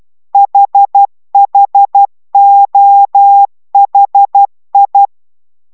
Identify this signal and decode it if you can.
Morse code, heard as a keyed tone, about 12 wpm, 800 Hz. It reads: HHOHI